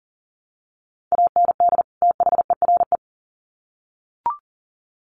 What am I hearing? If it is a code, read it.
Morse code, heard as a keyed tone, about 40 wpm, 700 Hz. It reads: AND THERE
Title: Sets of 2 Words 40wpm